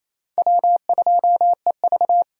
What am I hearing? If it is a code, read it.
Morse code, heard as a keyed tone, about 28 wpm, 700 Hz. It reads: W2EV
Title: USA Call Signs - 4 Characters - Repeated 2x - 28wpm